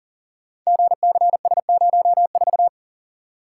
Morse code, heard as a keyed tone, about 38 wpm, 700 Hz. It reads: GCS04
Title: Code Groups - 5 Letter:Numbers - 40wpm